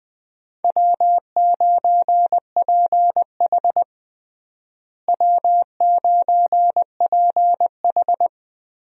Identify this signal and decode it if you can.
Morse code, heard as a keyed tone, about 21 wpm, 700 Hz. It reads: W9PH W9PH
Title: USA Call Signs - 4 Characters - Repeated 3x - 20wpm